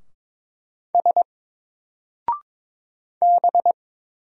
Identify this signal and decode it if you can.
Morse code, heard as a keyed tone, about 22 wpm, 700 Hz. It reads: S B
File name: Single-Letter-Number-22wpm